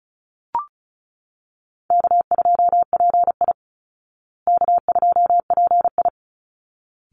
Morse code, heard as a keyed tone, about 34 wpm, 700 Hz. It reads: K2PI K2PI
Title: Easy Call Signs Repeated 2x - 35wpm